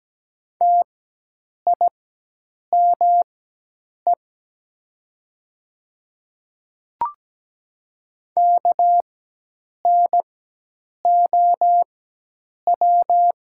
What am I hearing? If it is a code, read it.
Morse code, heard as a keyed tone, about 17 wpm, spaced as for 8 wpm, 700 Hz. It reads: TIME KNOW
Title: 4x Character Spacing - Top 100 Words - 17wpm